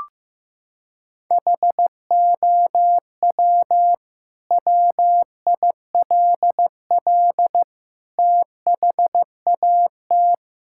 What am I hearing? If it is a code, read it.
Morse code, heard as a keyed tone, about 15 wpm, 700 Hz. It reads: HOW WILL THAT
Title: Sentences from Top 1000 Words - Review Part 6 - 15wpm